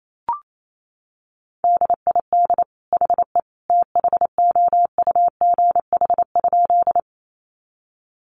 Morse code, heard as a keyed tone, about 28 wpm, 700 Hz. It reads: DID HE THOUGH?
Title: Sentences from Top 400 Words 28wpm